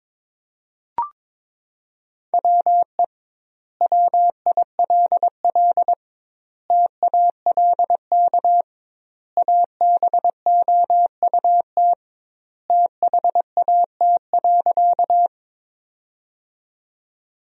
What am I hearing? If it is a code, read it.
Morse code, heard as a keyed tone, about 21 wpm, 700 Hz. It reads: WE WILL TALK ABOUT THAT.
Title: 2x Word-Spacing - Sentences from Top 400 Words - 22wpm